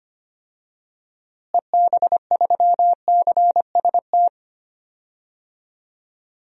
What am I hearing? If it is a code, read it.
Morse code, heard as a keyed tone, about 25 wpm, 700 Hz. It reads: EB3CST
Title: DXCC-100 Call Signs - 6 Characters - Repeated 2x - 25wpm